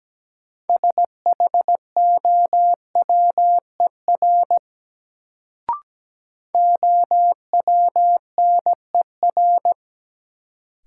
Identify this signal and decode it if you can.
Morse code, heard as a keyed tone, about 17 wpm, 700 Hz. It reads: SHOWER OWNER
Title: Sentences from Top 1600 Words - New Word Drill Alt - 17wpm